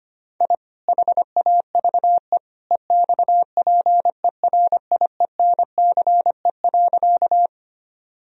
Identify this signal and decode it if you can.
Morse code, heard as a keyed tone, about 25 wpm, 700 Hz. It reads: I HAVE EXPERIENCE.
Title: Sentences from Top 1000 Words - Review Part 3 - 25wpm